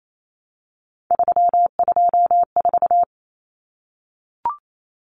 Morse code, heard as a keyed tone, about 28 wpm, 700 Hz. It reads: 324
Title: Common Cut Numbers - 3-Digit 28wpm